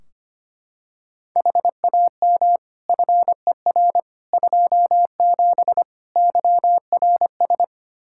Morse code, heard as a keyed tone, about 25 wpm, 700 Hz. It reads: HAM FER 27 YRS
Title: Practice QSO - Exchange 3 - More Info - 25wpm